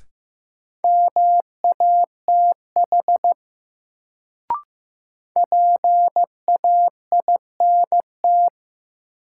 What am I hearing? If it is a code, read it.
Morse code, heard as a keyed tone, about 15 wpm, 700 Hz. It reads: MATH PAINT
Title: Sentences from Top 1000 Words - Review Part 6 - New Word Drill - 15wpm